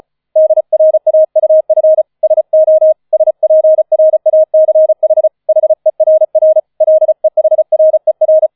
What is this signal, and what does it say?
Morse code, heard as a keyed tone, about 35 wpm, 630 Hz. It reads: DRAUF SO SPRACH HERR LEHRER